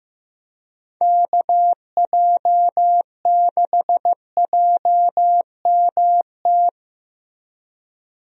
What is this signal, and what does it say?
Morse code, heard as a keyed tone, about 15 wpm, 700 Hz. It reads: KJ6JMT
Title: USA Call Signs - 6 Characters - 15wpm